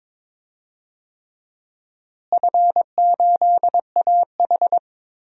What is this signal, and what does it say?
Morse code, heard as a keyed tone, about 22 wpm, 700 Hz. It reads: F8AH
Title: DXCC-100 Call Signs - 4 Characters - Repeated 3x - 22wpm